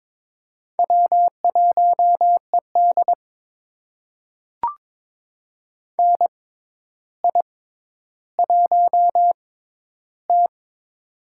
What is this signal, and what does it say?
Morse code, heard as a keyed tone, about 22 wpm, spaced as for 11 wpm, 700 Hz. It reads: W1ED NI1T
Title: USA Call Signs - 4 Characters - 6x Character Spacing - 22wpm